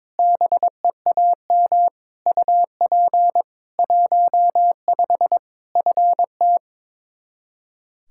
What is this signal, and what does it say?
Morse code, heard as a keyed tone, about 21 wpm, 700 Hz. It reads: BEAM UP 15 FT
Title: Practice QSO - Exchange 2 - Info - 22wpm